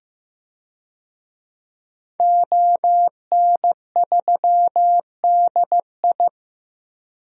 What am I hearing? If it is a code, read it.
Morse code, heard as a keyed tone, about 15 wpm, 700 Hz. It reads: ON3DI
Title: DXCC-100 Call Signs - 5 Characters - Repeated 3x - 15wpm